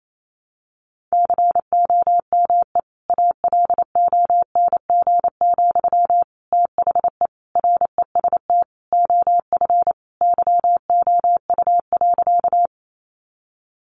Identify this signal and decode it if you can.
Morse code, heard as a keyed tone, about 28 wpm, 700 Hz. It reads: COME ALONG, THE REST OF YOU.
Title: Sentences from Top 700 Words 28wpm